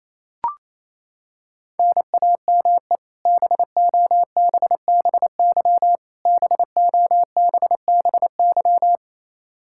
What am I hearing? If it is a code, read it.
Morse code, heard as a keyed tone, about 28 wpm, 700 Hz. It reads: NAME BOBBY BOBBY
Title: Practice QSO - Exchange 1 - Beginning - 28wpm